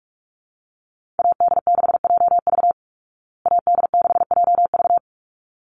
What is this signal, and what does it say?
Morse code, heard as a keyed tone, about 44 wpm, 700 Hz. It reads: AD6JV AD6JV
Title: Call Signs Repeated 3x - 45wpm